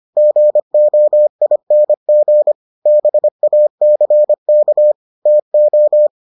Morse code, heard as a keyed tone, about 25 wpm, 600 Hz. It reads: GOING BACK TO